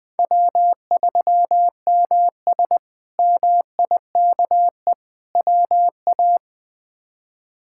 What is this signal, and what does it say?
Morse code, heard as a keyed tone, about 20 wpm, 700 Hz. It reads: W3MS MIKE WA
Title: K1USN SST QSO Practice - 20wpm